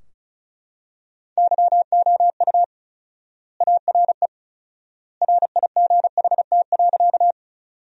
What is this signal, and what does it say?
Morse code, heard as a keyed tone, about 36 wpm, 700 Hz. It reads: YOU ARE RIGHT.
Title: 4x Word-Spacing - Sentences from Top 200 Words - 35wpm